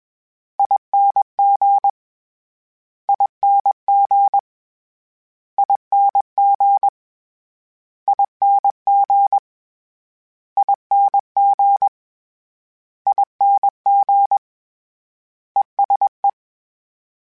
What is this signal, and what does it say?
Morse code, heard as a keyed tone, about 21 wpm, 800 Hz. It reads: ING ING ING ING ING ING ESE
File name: Short Runs / Short Runs-mp3s / ESE ING @20